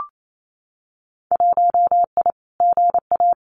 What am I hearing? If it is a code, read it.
Morse code, heard as a keyed tone, about 28 wpm, 700 Hz. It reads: 1I GA
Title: Winter Field Day Exchanges Encore 28wpm